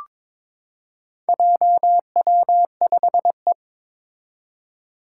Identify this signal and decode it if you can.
Morse code, heard as a keyed tone, about 22 wpm, 700 Hz. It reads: JW5E
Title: Four Challenging Letter-Number Combinations 22wpm